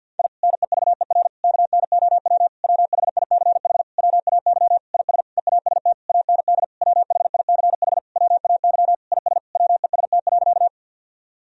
Morse code, heard as a keyed tone, about 52 wpm, 700 Hz. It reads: I NEVER KNOW WHICH WAY IS EAST AND WHICH WAY IS WEST.
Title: Sentences from Top 1000 Words - Review Part 3 - 50wpm